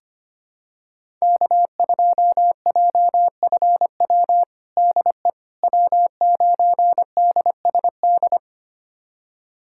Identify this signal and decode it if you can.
Morse code, heard as a keyed tone, about 25 wpm, 700 Hz. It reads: K2JFW DE W9DSD
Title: USA Callsign DE Callsign - 5 Characters - 25wpm